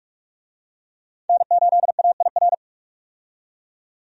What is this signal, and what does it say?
Morse code, heard as a keyed tone, about 44 wpm, 700 Hz. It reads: N8AIR
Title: USA Call Signs - 5 Characters - 45wpm